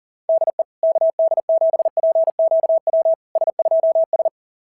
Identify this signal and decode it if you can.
Morse code, heard as a keyed tone, about 37 wpm, 690 Hz. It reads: DE KD7PQW S2S
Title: SOTA QSO Practice 40wpm